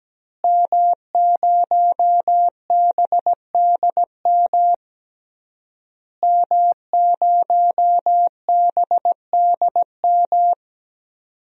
Morse code, heard as a keyed tone, about 17 wpm, 700 Hz. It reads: M0BDM M0BDM
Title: DXCC-100 Call Signs - 5 Characters - Repeated 3x - 17wpm